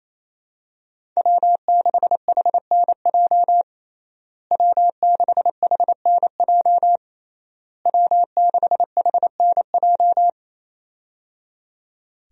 Morse code, heard as a keyed tone, about 27 wpm, 700 Hz. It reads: W6HNJ W6HNJ W6HNJ
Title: USA Call Signs - 5 Characters - Repeated 3x - 28wpm